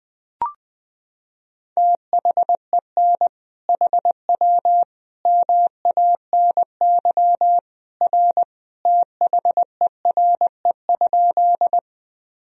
Morse code, heard as a keyed tone, about 21 wpm, 700 Hz. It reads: THEN HW MANY R THERE?
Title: Sentences from Top 100 Words and Abbreviations 20wpm